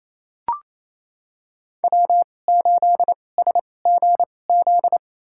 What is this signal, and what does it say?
Morse code, heard as a keyed tone, about 28 wpm, spaced as for 20 wpm, 700 Hz. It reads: W8SGZ
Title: USA Call Signs - 5 Characters - 2x Character Spacing - 28wpm